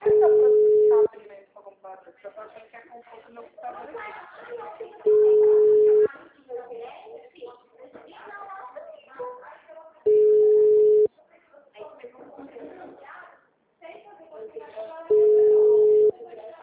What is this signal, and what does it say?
Ringback tone (Europe, 425 Hz), with speech.